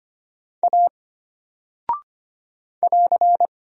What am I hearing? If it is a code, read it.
Morse code, heard as a keyed tone, about 25 wpm, 700 Hz. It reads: A <AR>
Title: Single Character (Letter:Number:Punctuation:Prosign) Alt 25wpm